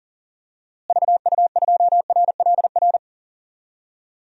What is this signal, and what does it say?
Morse code, heard as a keyed tone, about 40 wpm, 700 Hz. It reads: VU2RLR
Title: DXCC-100 Call Signs - 6 Characters - Repeated 3x - 40wpm